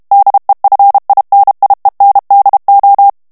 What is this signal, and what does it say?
Morse code, heard as a keyed tone, about 32 wpm, 800 Hz. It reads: DEFINIENDO
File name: ESPAGNOL TEXTE EN CLAIR 2 VITESSE 30 MOTS MINUTES